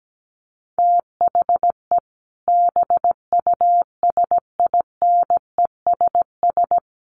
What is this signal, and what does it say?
Morse code, heard as a keyed tone, about 17 wpm, 700 Hz. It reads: THE BUSINESS
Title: Sentences from Top 1400 Words 17wpm